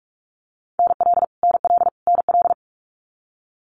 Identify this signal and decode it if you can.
Morse code, heard as a keyed tone, about 44 wpm, 700 Hz. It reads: NL NL NL
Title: Canadian Provinces and Territories - Repeated 3x - 45wpm